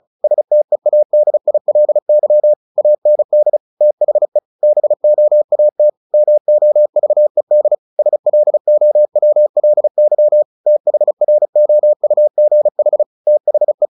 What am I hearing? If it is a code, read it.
Morse code, heard as a keyed tone, about 34 wpm, 600 Hz. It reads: STEADILY AND THE BOAT MOVED SLOWLY THROUGH THE
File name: the old man and the sea / the old man and the sea 1 35wpm